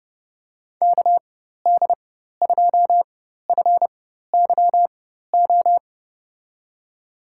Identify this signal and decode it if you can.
Morse code, heard as a keyed tone, about 31 wpm, spaced as for 14 wpm, 700 Hz. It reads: KD2FYO
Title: USA Call Signs - 4x Character Spacing - 30wpm